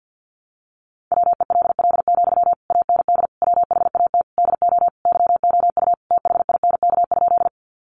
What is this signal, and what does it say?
Morse code, heard as a keyed tone, about 48 wpm, 700 Hz. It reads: WELL, AND WHAT DO YOU THINK?
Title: Sentences from Top 200 Words 50wpm